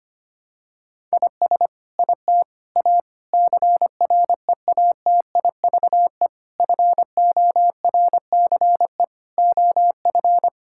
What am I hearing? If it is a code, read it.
Morse code, heard as a keyed tone, about 25 wpm, 700 Hz. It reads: IS IT A CREATIVE FORCE OF
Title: Sentences from Top 1000 Words - Review Part 5 - 25wpm